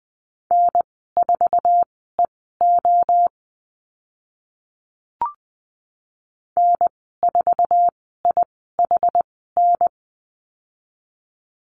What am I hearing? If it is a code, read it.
Morse code, heard as a keyed tone, about 20 wpm, spaced as for 15 wpm, 700 Hz. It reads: N4EO N4IHN
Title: 076 Lesson 13g - Letter L - Call Signs 20wpm